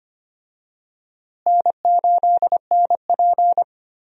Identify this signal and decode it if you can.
Morse code, heard as a keyed tone, about 25 wpm, 700 Hz. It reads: N8NP
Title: USA Call Signs - 4 Characters - Repeated 3x - 25wpm